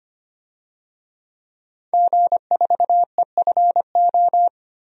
Morse code, heard as a keyed tone, about 25 wpm, 700 Hz. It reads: G4EFO
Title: DXCC-100 Call Signs - 5 Characters - 25wpm